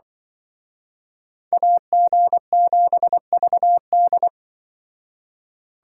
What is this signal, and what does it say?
Morse code, heard as a keyed tone, about 25 wpm, 700 Hz. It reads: AG7VD
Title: Speed Racing - USA Call Signs 20wpm